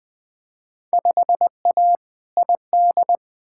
Morse code, heard as a keyed tone, about 21 wpm, 700 Hz. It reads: 5A ID
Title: Field Day Exchanges - 20wpm